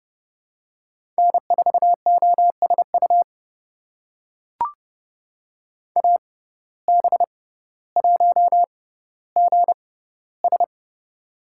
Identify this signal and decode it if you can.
Morse code, heard as a keyed tone, about 30 wpm, spaced as for 15 wpm, 700 Hz. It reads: N4OSU AB1GS
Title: USA Call Signs - 5 Characters - 6x Character Spacing - 30wpm